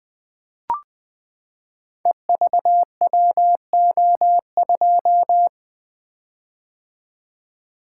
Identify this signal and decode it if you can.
Morse code, heard as a keyed tone, about 21 wpm, 700 Hz. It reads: EVWO2
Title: Code Groups - 5 Letter:Numbers - 20wpm